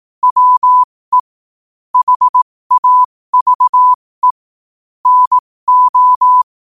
Morse code, heard as a keyed tone, about 18 wpm, spaced as for 15 wpm, 1000 Hz. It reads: WE HAVE NO
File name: qotdfiles / 15wpm